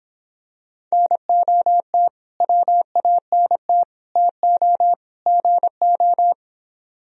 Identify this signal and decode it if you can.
Morse code, heard as a keyed tone, about 26 wpm, 700 Hz. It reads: NOT WANT TO GO